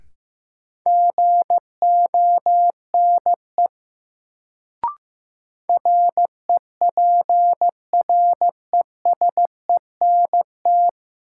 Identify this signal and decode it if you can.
Morse code, heard as a keyed tone, about 15 wpm, 700 Hz. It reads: GONE REPRESENT
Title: Sentences from Top 1200 Words - New Word Drill Alt - 15wpm